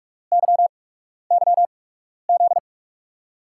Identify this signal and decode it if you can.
Morse code, heard as a keyed tone, about 44 wpm, spaced as for 12 wpm, 700 Hz. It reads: YYZ
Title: USA Call Signs - 6 Characters - 8x Character Spacing - 45wpm